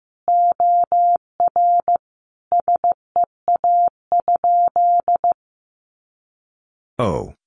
Morse code, heard as a keyed tone, about 14 wpm, 700 Hz. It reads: OR SEA?
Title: Sentences from Top 500 Words 15wpm